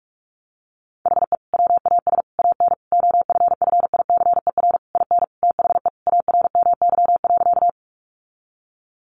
Morse code, heard as a keyed tone, about 44 wpm, 700 Hz. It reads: HE WAS AN OFFICER IN THE ARMY.
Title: Sentences from Top 1200 Words 45wpm